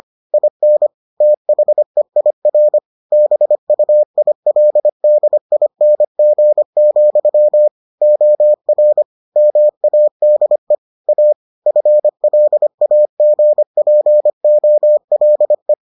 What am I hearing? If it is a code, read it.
Morse code, heard as a keyed tone, about 25 wpm, 600 Hz. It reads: IN THEIR BUILDING, OR MADE A FLAGPOLE